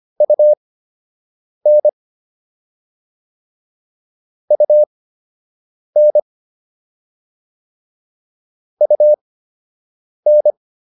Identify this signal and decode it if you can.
Morse code, heard as a keyed tone, about 25 wpm, spaced as for 7 wpm, 600 Hz. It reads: UN UN UN